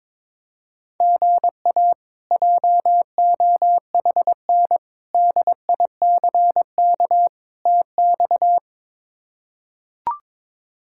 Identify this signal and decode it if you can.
Morse code, heard as a keyed tone, about 22 wpm, 700 Hz. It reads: GA JOHN DICK TX
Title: K1USN SST QSO Practice - 22wpm